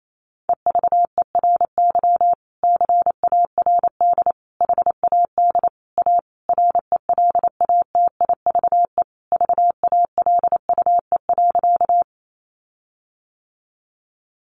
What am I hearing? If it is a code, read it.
Morse code, heard as a keyed tone, about 28 wpm, 700 Hz. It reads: EVERY CARD HAD A RELATIVE VALUE.
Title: Sentences from Top 1400 Words 28wpm